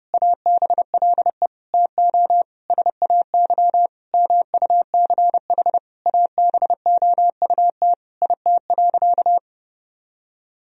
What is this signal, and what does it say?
Morse code, heard as a keyed tone, about 30 wpm, 700 Hz. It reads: ABLE TO SAY MUCH ABOUT IT.
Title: Sentences from Top 1000 Words - Review Part 2 - 30wpm